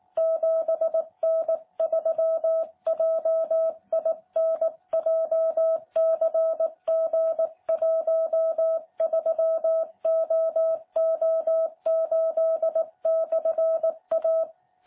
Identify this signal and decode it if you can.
Morse code, heard as a keyed tone, about 18 wpm, 660 Hz. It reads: 7N3JIN JCG13OO8/A
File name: jh8cbh / 2021cw